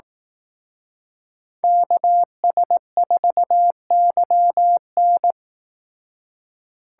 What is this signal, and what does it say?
Morse code, heard as a keyed tone, about 18 wpm, 700 Hz. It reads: KS4YN
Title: Speed Racing - USA 2x2 Call Signs 15wpm